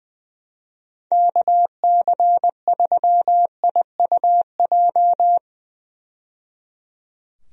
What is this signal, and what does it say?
Morse code, heard as a keyed tone, about 21 wpm, 700 Hz. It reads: KC3IUJ
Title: USA Call Signs - 6 Characters - 20wpm